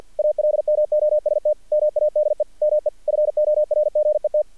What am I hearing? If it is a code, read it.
Morse code, heard as a keyed tone, about 48 wpm, 600 Hz. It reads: ALMOST MADE ME FORGET
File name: archived / CRSnet